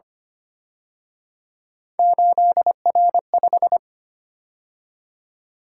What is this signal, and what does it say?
Morse code, heard as a keyed tone, about 29 wpm, 700 Hz. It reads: 8R5
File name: Speed-Racing - Three Letter-Number Combinations - 25wpm